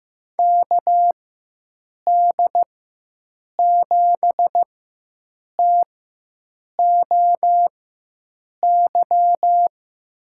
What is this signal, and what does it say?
Morse code, heard as a keyed tone, about 15 wpm, spaced as for 7 wpm, 700 Hz. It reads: KD7TOY
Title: USA Call Signs - 4x Character Spacing - 15wpm